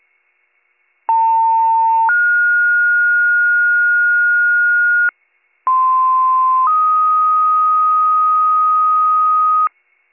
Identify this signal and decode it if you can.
Two-tone page, heard as a one-second tone, then a three-second tone, about 900 then 1430 Hz, about 1040 then 1250 Hz, with noise. It is heard from 1.1 to 9.7 s.